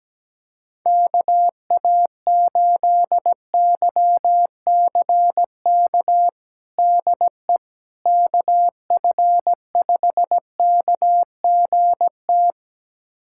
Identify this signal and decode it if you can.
Morse code, heard as a keyed tone, about 17 wpm, 700 Hz. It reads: KA8YCK DE KF5KGT
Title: USA Callsign DE Callsign - 6 Characters - 17wpm